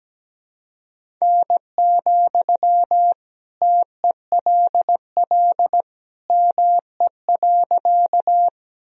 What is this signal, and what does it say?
Morse code, heard as a keyed tone, about 17 wpm, 700 Hz. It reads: N, TELL ME.
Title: Sentences from Top 200 Words and Abbreviations 17wpm